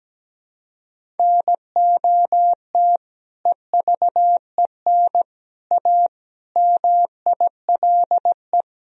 Morse code, heard as a keyed tone, about 17 wpm, 700 Hz. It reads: NOT EVEN A MILE
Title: Sentences from Top 1000 Words - Review Part 1 - 17wpm